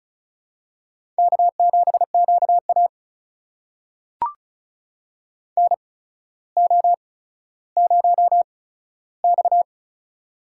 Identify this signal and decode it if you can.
Morse code, heard as a keyed tone, about 36 wpm, spaced as for 15 wpm, 700 Hz. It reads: K7QA NO0X
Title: USA Call Signs - 4 Characters - 8x Character Spacing - 35wpm